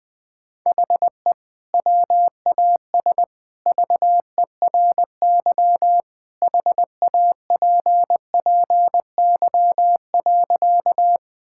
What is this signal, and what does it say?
Morse code, heard as a keyed tone, about 20 wpm, 700 Hz. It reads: HE WAS VERY HAPPY.
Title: Sentences with 2-Word Phrases - 20wpm